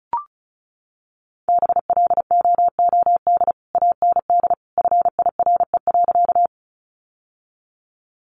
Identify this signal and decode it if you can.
Morse code, heard as a keyed tone, about 34 wpm, 700 Hz. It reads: BLOOD AND FIRE.
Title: Sentences from Top 1100 Words 35wpm